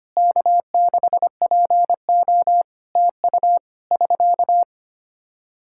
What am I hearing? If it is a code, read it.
Morse code, heard as a keyed tone, about 25 wpm, 700 Hz. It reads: K6PO TU <SK>
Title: Practice QSO - Exchange 4 - Ending - 25wpm